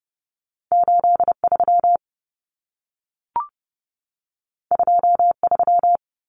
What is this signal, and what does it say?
Morse code, heard as a keyed tone, about 30 wpm, 700 Hz. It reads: 83 23
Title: Numbers - 2-Digit 30wpm